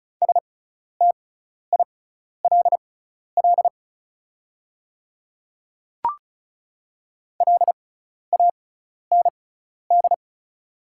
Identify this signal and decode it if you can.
Morse code, heard as a keyed tone, about 36 wpm, spaced as for 12 wpm, 700 Hz. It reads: STILL LAND
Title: 6x Character Spacing - Top 300 Words - 35wpm